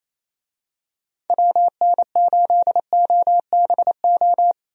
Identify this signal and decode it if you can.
Morse code, heard as a keyed tone, about 28 wpm, 700 Hz. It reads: WN8OBO
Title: USA Call Signs - 6 Characters - 28wpm